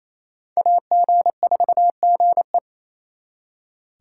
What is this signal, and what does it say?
Morse code, heard as a keyed tone, about 35 wpm, 700 Hz. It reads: AG4GE
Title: Speed Racing - USA 2x2 Call Signs 28wpm